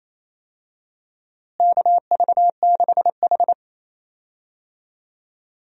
Morse code, heard as a keyed tone, about 28 wpm, 700 Hz. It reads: KV6H
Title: USA Call Signs - 4 Characters - Repeated 3x - 28wpm